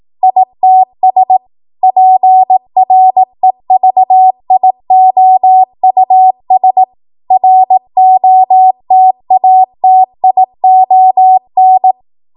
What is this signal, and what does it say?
Morse code, heard as a keyed tone, about 18 wpm, 750 Hz. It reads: ITS PREVIOUS ROTATION
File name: Morse / 18WPM